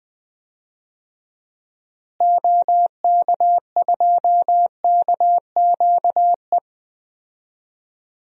Morse code, heard as a keyed tone, about 20 wpm, 700 Hz. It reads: OK2KQE
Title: DXCC-100 Call Signs - 6 Characters - Repeated 2x - 20wpm